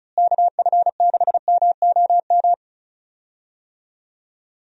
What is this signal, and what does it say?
Morse code, heard as a keyed tone, about 36 wpm, 700 Hz. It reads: KF6MOM
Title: USA Call Signs - 6 Characters - Repeated 3x - 35wpm